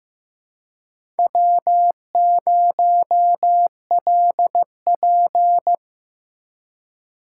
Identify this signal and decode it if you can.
Morse code, heard as a keyed tone, about 15 wpm, 700 Hz. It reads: W0LP
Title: USA Call Signs - 4 Characters - 2x Character Spacing - 15wpm